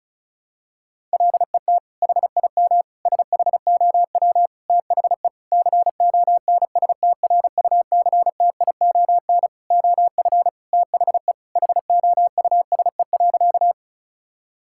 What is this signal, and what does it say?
Morse code, heard as a keyed tone, about 36 wpm, 700 Hz. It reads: LET HIM SHOW THE CONSTRUCTION OF THE HOUSE.
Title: Sentences from Top 1600 Words 35wpm